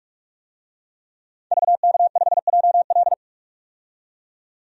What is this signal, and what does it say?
Morse code, heard as a keyed tone, about 43 wpm, 700 Hz. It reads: VK5JL
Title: DXCC-100 Call Signs - 5 Characters - 45wpm